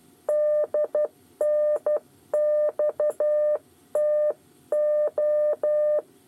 Morse code, heard as a keyed tone, about 12 wpm, 600 Hz. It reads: DNXTO
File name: 8 bis lezione cw 2